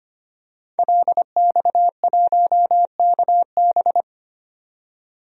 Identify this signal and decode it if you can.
Morse code, heard as a keyed tone, about 25 wpm, 700 Hz. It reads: LX1KB
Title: DXCC-100 Call Signs - 5 Characters - 25wpm